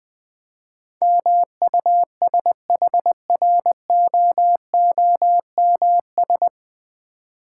Morse code, heard as a keyed tone, about 20 wpm, 700 Hz. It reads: MUSHROOMS
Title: Difficult Words with Hints - 20wpm